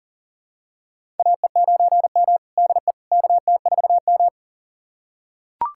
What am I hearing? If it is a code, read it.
Morse code, heard as a keyed tone, about 40 wpm, 700 Hz. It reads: AE9M DE KT4M
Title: USA Callsign DE Callsign - 4 Characters - 40wpm